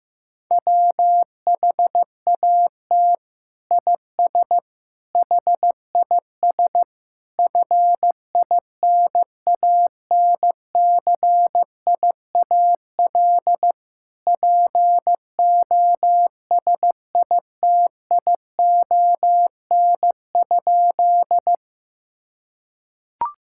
Morse code, heard as a keyed tone, about 15 wpm, 700 Hz. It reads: WHAT IS HIS FINANCIAL POSITION?
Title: Sentences from Top 1000 Words - Review Part 5 - 15wpm